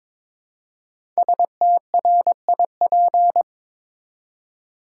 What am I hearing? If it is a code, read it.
Morse code, heard as a keyed tone, about 22 wpm, 700 Hz. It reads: STRIP
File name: 5 Letter Words - First 3 Letters Given - 22wpm